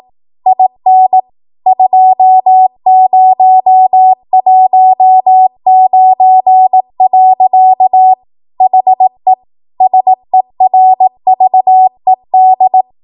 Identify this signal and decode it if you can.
Morse code, heard as a keyed tone, about 18 wpm, 750 Hz. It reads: IN 2019. HE SERVED